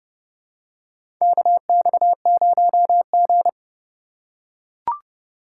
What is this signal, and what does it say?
Morse code, heard as a keyed tone, about 31 wpm, 700 Hz. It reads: KX0G
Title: USA Call Signs - 2x Character Spacing - 30wpm